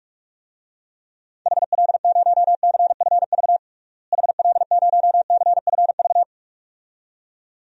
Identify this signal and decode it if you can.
Morse code, heard as a keyed tone, about 46 wpm, 700 Hz. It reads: HL0CFV HL0CFV
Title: DXCC-100 Call Signs - 6 Characters - Repeated 2x - 45wpm